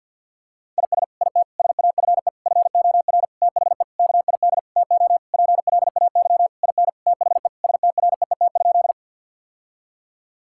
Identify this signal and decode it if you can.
Morse code, heard as a keyed tone, about 50 wpm, 700 Hz. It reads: IS IT SAFE FOR THE KID TO PLAY IN THE STREET?
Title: Sentences from Top 1600 Words 50wpm